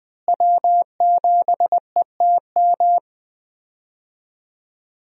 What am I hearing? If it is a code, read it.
Morse code, heard as a keyed tone, about 20 wpm, 700 Hz. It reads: W7ETM
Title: USA Call Signs - 5 Characters - Repeated 2x - 20wpm